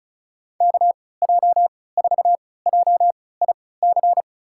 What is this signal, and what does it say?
Morse code, heard as a keyed tone, about 36 wpm, spaced as for 20 wpm, 700 Hz. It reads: KJ4JIC
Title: USA Call Signs - 6 Characters - 3x Character Spacing - 35wpm